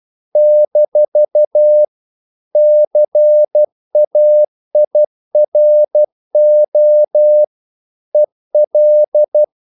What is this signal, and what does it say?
Morse code, heard as a keyed tone, about 12 wpm, 600 Hz. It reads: - CAIRO EL
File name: CW / Practice / NationalCapitals12